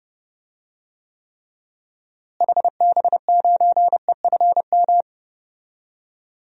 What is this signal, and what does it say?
Morse code, heard as a keyed tone, about 31 wpm, 700 Hz. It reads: HB9EFM
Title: DXCC-100 Call Signs - 6 Characters - Repeated 2x - 30wpm